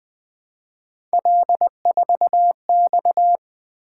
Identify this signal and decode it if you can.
Morse code, heard as a keyed tone, about 20 wpm, 700 Hz. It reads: L4X
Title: Three Letter-Number Combinations - 20wpm